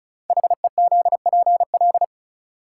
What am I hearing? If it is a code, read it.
Morse code, heard as a keyed tone, about 36 wpm, 700 Hz. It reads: HEZPL
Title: Code Groups - 5 Letters - 35wpm